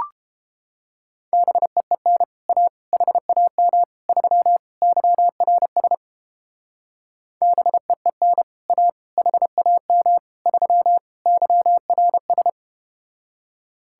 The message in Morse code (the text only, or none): BEEN A HAM 3 YRS BEEN A HAM 3 YRS